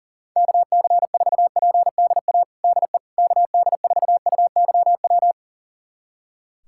Morse code, heard as a keyed tone, about 40 wpm, 690 Hz. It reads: KC4PDA DE KD4UYW
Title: USA Callsign DE Callsign - 6 Characters - 40wpm